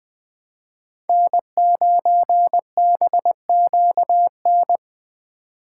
Morse code, heard as a keyed tone, about 20 wpm, 700 Hz. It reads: N9BQN
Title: Speed Racing - USA 1x3 Call Signs 17wpm